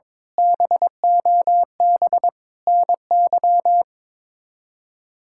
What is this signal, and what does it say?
Morse code, heard as a keyed tone, about 21 wpm, 700 Hz. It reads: BOB NY
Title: K1USN SST QSO Practice - 22wpm